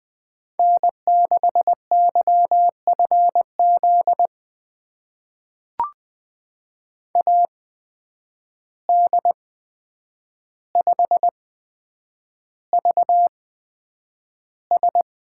Morse code, heard as a keyed tone, about 20 wpm, spaced as for 9 wpm, 700 Hz. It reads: N6YFZ AD5VS
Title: USA Call Signs - 8x Character Spacing - 20wpm